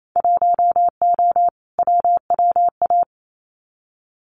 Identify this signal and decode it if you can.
Morse code, heard as a keyed tone, about 28 wpm, 700 Hz. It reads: 1O WWA
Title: Winter Field Day Exchanges 28wpm